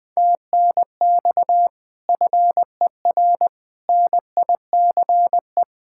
Morse code, heard as a keyed tone, about 21 wpm, 700 Hz. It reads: TNX FER NICE
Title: Speed-Racing Common QSO Phrases Encore - 20wpm